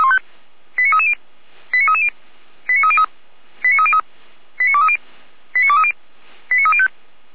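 Five-tone selective call (ZVEI), heard as quick runs of five tones, tones of about 70 ms. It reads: 89330 89330 89303 89303 89230 89230 89305
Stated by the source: ZVEI five-tone selective call received by SDR, from the fire brigade Feuerwehr Orlamünde.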